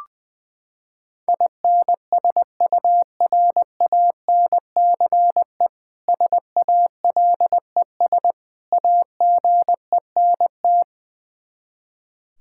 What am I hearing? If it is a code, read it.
Morse code, heard as a keyed tone, about 20 wpm, 700 Hz. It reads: INSURANCE SALES AGENT
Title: K1USN SST QSO Practice 20wpm